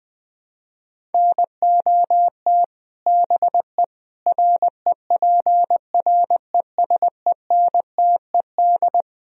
Morse code, heard as a keyed tone, about 21 wpm, 700 Hz. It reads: NOT BE REPRESENTED
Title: Speed-Racing - Sets of 5 Words 20wpm